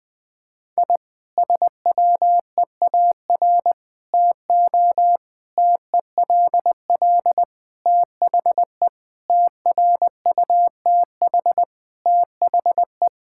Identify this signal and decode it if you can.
Morse code, heard as a keyed tone, about 21 wpm, 700 Hz. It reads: I SWEAR TO TELL THE TRUTH THE
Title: Sentences with 9-Word Phrases - Spoken Beginning - 20wpm